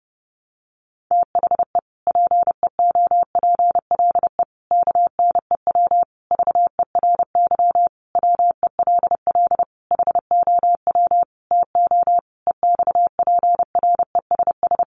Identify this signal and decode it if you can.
Morse code, heard as a keyed tone, about 30 wpm, 700 Hz. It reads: THE PEOPLE KNEW VERY WELL HOW TO EXPRESS
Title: Sentences from Top 1100 Words 30wpm